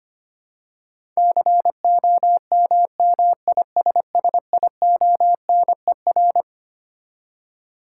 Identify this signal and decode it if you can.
Morse code, heard as a keyed tone, about 25 wpm, 700 Hz. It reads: COMMISSIONER
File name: Twelve Letter Words - 25wpm